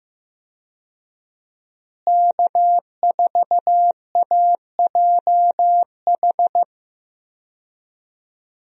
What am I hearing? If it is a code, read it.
Morse code, heard as a keyed tone, about 15 wpm, 700 Hz. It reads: K4AJH
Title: USA Call Signs - 5 Characters - Repeated 2x - 15wpm